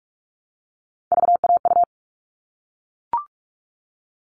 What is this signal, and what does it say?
Morse code, heard as a keyed tone, about 44 wpm, 700 Hz. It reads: VAU
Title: All Cut Numbers - 3-Digit 45wpm